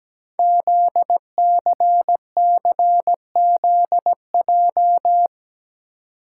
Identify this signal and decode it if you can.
Morse code, heard as a keyed tone, about 17 wpm, 700 Hz. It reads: ZCCZJ
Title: Code Groups - 5 Letters - 17wpm